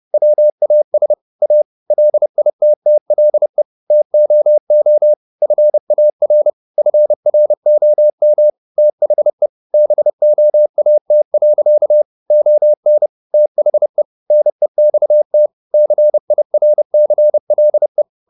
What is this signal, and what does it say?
Morse code, heard as a keyed tone, about 30 wpm, 600 Hz. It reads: WAS A LITTLE TOO FAR FROM THE BOAT. ON THE NEXT CIRCLE